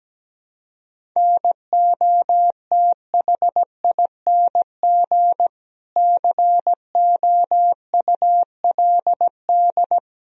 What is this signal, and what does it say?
Morse code, heard as a keyed tone, about 17 wpm, 700 Hz. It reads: NOTHING COULD
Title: Sentences from Top 500 Words and Abbreviations 17wpm